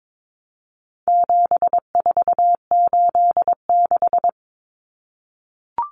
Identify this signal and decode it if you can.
Morse code, heard as a keyed tone, about 22 wpm, 700 Hz. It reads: 7486
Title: Common Cut Numbers - 4-Digit 22wpm